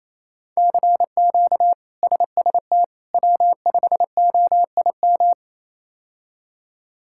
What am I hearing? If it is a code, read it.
Morse code, heard as a keyed tone, about 27 wpm, 700 Hz. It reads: CQ SST W5OIM
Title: K1USN SST QSO Practice - 28wpm